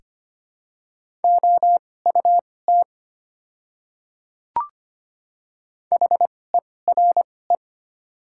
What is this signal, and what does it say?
Morse code, heard as a keyed tone, about 25 wpm, spaced as for 18 wpm, 700 Hz. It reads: OUT HERE
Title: 2x Character Spacing - Top 200 Words - 25wpm